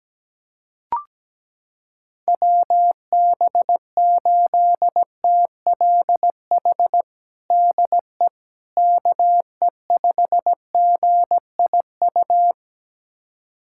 Morse code, heard as a keyed tone, about 17 wpm, 700 Hz. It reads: WB8TLH DE KE5GIU
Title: USA Callsign DE Callsign - 6 Characters - 17wpm